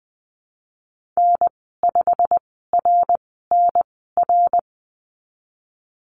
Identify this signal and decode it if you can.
Morse code, heard as a keyed tone, about 20 wpm, spaced as for 15 wpm, 700 Hz. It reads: N5RNR
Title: 179 Lesson 29g - Slash - Call Signs 20wpm